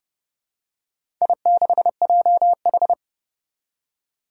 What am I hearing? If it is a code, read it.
Morse code, heard as a keyed tone, about 30 wpm, 700 Hz. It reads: I6JH
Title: Four Challenging Letter-Number Combinations 30wpm